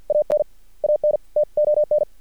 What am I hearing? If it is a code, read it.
Morse code, heard as a keyed tone, about 50 wpm, 600 Hz. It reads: AN AN TON
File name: archived / CRSnet